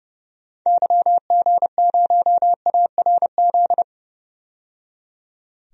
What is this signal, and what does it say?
Morse code, heard as a keyed tone, about 30 wpm, 700 Hz. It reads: YG0ARZ
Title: DXCC-100 Call Signs - 6 Characters - 30wpm